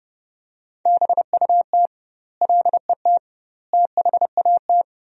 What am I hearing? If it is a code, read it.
Morse code, heard as a keyed tone, about 30 wpm, 700 Hz. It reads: BUT LET THAT
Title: 2x Word-Spacing - Sentences from Top 400 Words - 30wpm